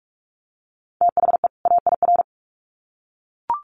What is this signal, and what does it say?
Morse code, heard as a keyed tone, about 44 wpm, 700 Hz. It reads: THE AIR
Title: Sets of 2 Words 45wpm